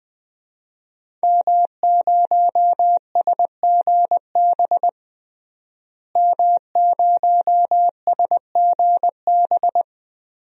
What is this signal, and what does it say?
Morse code, heard as a keyed tone, about 20 wpm, 700 Hz. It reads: M0SGB M0SGB
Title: DXCC-100 Call Signs - 5 Characters - Repeated 3x - 20wpm